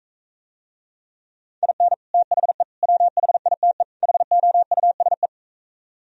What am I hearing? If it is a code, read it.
Morse code, heard as a keyed tone, about 45 wpm, 700 Hz. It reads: IN THE WHITE HOUSE